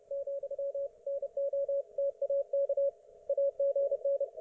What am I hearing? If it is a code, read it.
Morse code, heard as a keyed tone, about 29 wpm, 560 Hz. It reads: , NO TAK AGN